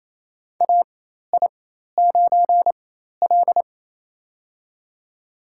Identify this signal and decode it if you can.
Morse code, heard as a keyed tone, about 28 wpm, spaced as for 13 wpm, 700 Hz. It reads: AI9L
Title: USA Call Signs - 4 Characters - 4x Character Spacing - 28wpm